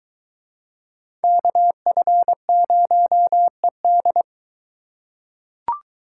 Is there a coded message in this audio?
Yes: Morse code from 1.2 to 4.2 s, heard as a keyed tone, about 21 wpm, 700 Hz. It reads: KF0ED